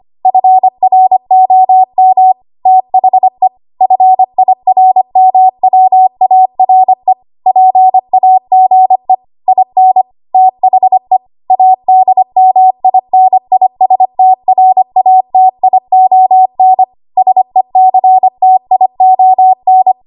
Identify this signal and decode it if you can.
Morse code, heard as a keyed tone, about 25 wpm, 750 Hz. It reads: FROM THE FIRMWARE PAGE IN THE ADMINISTRATION SECTION
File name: Archive / 25 WPM / 25WPM